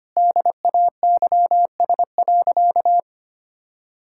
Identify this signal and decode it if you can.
Morse code, heard as a keyed tone, about 25 wpm, 700 Hz. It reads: DAYS.
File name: practice / 25wpm / Sentences with 4-Word Phrases - 25wpm